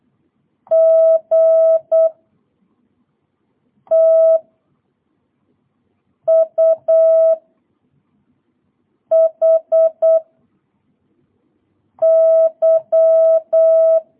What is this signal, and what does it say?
Morse code, heard as a keyed tone, about 8 wpm, 660 Hz. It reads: G T U H Y